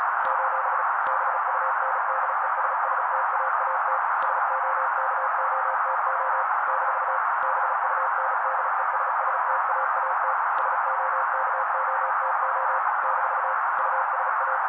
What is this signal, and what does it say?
Morse code, heard as a keyed tone, about 35 wpm, 530 Hz. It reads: : BUNDESSTAAT SOKOTO - BUNDESSTAAT SOKOTO - USA
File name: daily / 35wpm noise